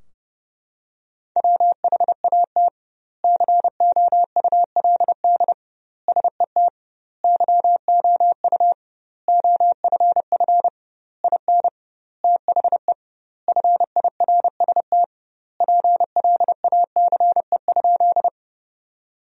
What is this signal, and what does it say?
Morse code, heard as a keyed tone, about 32 wpm, 700 Hz. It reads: WHAT COULD SET YOU OFF IN THE FIRST PLACE?
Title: 2x Word-Spacing - Sentences from Top 200 Words - 30wpm